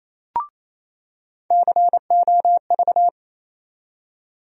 Morse code, heard as a keyed tone, about 28 wpm, 700 Hz. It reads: COV
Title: USA Call Sign Postfixes - 3 Letters 28wpm